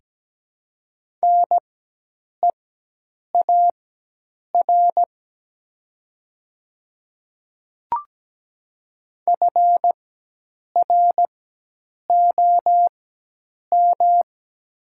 Morse code, heard as a keyed tone, about 17 wpm, spaced as for 8 wpm, 700 Hz. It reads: NEAR FROM
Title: 4x Character Spacing - Top 200 Words - 17wpm